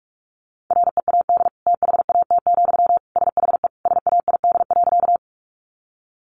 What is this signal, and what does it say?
Morse code, heard as a keyed tone, about 44 wpm, 700 Hz. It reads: READ THAT, SHE SAID.
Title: Sentences from Top 200 Words 45wpm